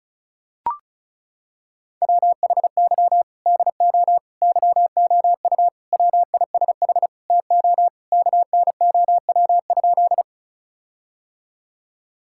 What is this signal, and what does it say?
Morse code, heard as a keyed tone, about 34 wpm, 700 Hz. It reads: WHY DO YOU WISH TO KNOW?
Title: Sentences from Top 1600 Words 35wpm